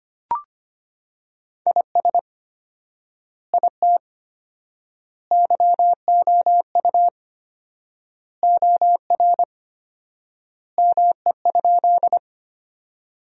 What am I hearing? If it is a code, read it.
Morse code, heard as a keyed tone, about 25 wpm, 700 Hz. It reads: IS IT YOU OR ME?
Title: 4x Word-Spacing - Sentences from Top 200 Words - 25wpm